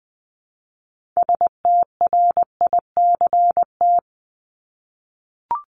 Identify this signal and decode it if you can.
Morse code, heard as a keyed tone, about 20 wpm, 700 Hz. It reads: STRICT
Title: Top 1501-1750 Words 20wpm